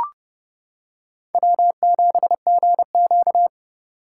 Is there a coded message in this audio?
Yes: Morse code from 1.3 to 3.5 s, heard as a keyed tone, about 30 wpm, 700 Hz. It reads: W7GQ